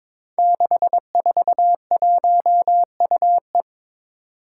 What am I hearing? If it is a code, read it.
Morse code, heard as a keyed tone, about 22 wpm, 700 Hz. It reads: 641UE
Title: Five Challenging Letter-Number Combinations 22wpm